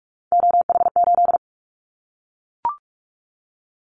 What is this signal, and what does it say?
Morse code, heard as a keyed tone, about 44 wpm, 700 Hz. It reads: OH8
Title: Callsign Prefixes for Top 25 DXCC - Repeated 3x - 45wpm